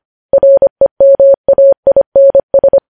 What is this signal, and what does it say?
Morse code, heard as a keyed tone, about 25 wpm, 550 Hz. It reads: REMAINS
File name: podcast / News headlines-1x-25x25